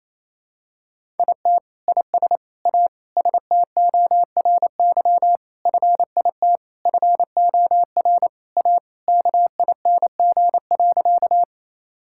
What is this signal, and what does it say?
Morse code, heard as a keyed tone, about 28 wpm, 700 Hz. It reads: IT IS A STORY FIT FOR A KING.
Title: Sentences from Top 1000 Words - Review Part 5 - 28wpm